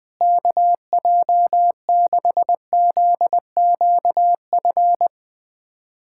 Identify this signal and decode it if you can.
Morse code, heard as a keyed tone, about 20 wpm, 700 Hz. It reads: KJ6ZQF
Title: USA Callsign DE Callsign 20wpm